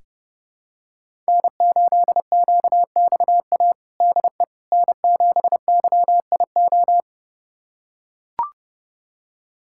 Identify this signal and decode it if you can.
Morse code, heard as a keyed tone, about 31 wpm, 700 Hz. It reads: N8QXA DE N7YIO
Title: USA Callsign DE Callsign - 5 Characters - 30wpm